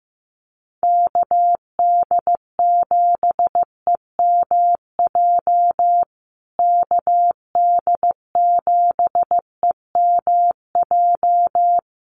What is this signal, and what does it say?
Morse code, heard as a keyed tone, about 15 wpm, 700 Hz. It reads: KD7EMJ KD7EMJ
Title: USA Call Signs Repeated 3x 15wpm